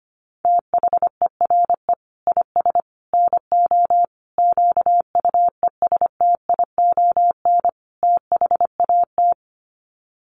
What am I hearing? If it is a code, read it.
Morse code, heard as a keyed tone, about 25 wpm, 700 Hz. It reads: THERE IS NO QUESTION THAT
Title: Sets of 5 Words Encore 25wpm